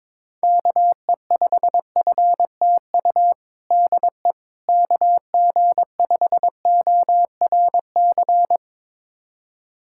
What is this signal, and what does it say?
Morse code, heard as a keyed tone, about 22 wpm, 700 Hz. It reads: KE5FTU DE KG5ORC
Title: USA Callsign DE Callsign 22wpm